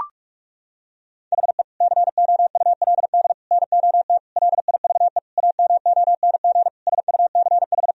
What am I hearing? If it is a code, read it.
Morse code, heard as a keyed tone, about 41 wpm, 700 Hz. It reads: HE COULD NOT LIVE AMONG SUCH
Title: Sentences from Top 1000 Words - Review Part 3 - 45wpm